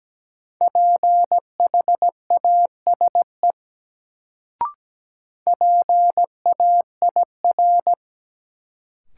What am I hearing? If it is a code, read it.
Morse code, heard as a keyed tone, about 17 wpm, 700 Hz. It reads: PHASE PAIR
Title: Sentences from Top 1200 Words - New Word Drill - 17wpm